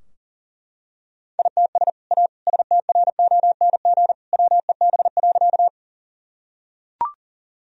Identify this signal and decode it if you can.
Morse code, heard as a keyed tone, about 40 wpm, 700 Hz. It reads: ITS A STRONG WEB.
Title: Sentences from Top 1000 Words - Review Part 4 - 40wpm